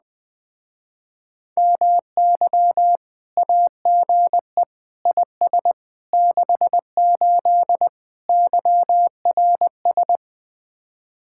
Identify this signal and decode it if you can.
Morse code, heard as a keyed tone, about 20 wpm, 700 Hz. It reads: MY AGE IS 68 YRS
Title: Speed-Racing Common QSO Phrases Encore - 17wpm